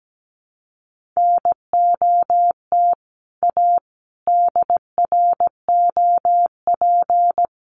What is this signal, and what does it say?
Morse code, heard as a keyed tone, about 17 wpm, 700 Hz. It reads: NOT A DROP
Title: Sentences from Top 1400 Words 17wpm